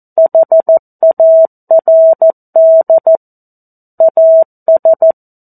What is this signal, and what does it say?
Morse code, heard as a keyed tone, about 14 wpm, 650 Hz. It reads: HARD AS
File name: 02-Chapter-02 14 WPM